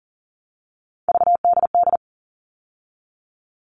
Morse code, heard as a keyed tone, about 40 wpm, 700 Hz. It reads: VDD
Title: All Cut Numbers - 3-Digit 40wpm